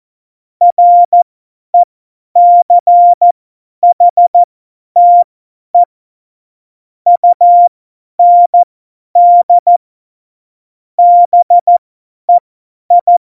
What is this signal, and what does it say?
Morse code, heard as a keyed tone, about 14 wpm, spaced as for 10 wpm, 700 Hz. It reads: RECHTE UND BEI
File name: Morsekurs / Witze / Witz 14 Juli 2020, 50 F70